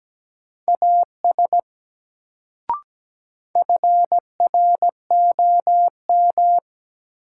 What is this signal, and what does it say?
Morse code, heard as a keyed tone, about 17 wpm, 700 Hz. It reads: AS FROM
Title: Sentences from Top 100 Words - New Word Drill - 17wpm